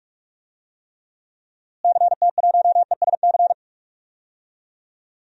Morse code, heard as a keyed tone, about 48 wpm, 700 Hz. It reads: CT1ESC
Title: DXCC-100 Call Signs - 6 Characters - Repeated 2x - 45wpm